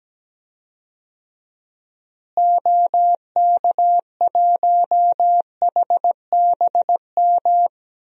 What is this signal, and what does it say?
Morse code, heard as a keyed tone, about 17 wpm, 700 Hz. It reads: OK1HBM
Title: DXCC-100 Call Signs - 6 Characters - Repeated 2x - 17wpm